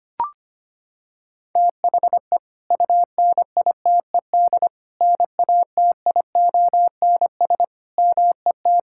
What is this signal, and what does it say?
Morse code, heard as a keyed tone, about 25 wpm, 700 Hz. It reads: THE UNITED NATIONS MET
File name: practice / 25wpm / Sentences with 3-Word Phrases - 25wpm